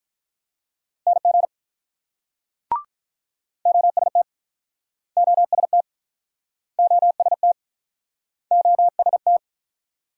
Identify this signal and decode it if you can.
Morse code, heard as a keyed tone, about 47 wpm, 690 Hz. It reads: NG OST OST OST OST